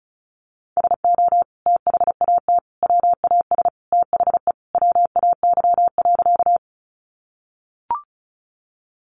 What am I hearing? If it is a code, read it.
Morse code, heard as a keyed tone, about 34 wpm, 700 Hz. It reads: SO THAT WAS THE WAY.
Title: Sentences from Top 100 Words 35wpm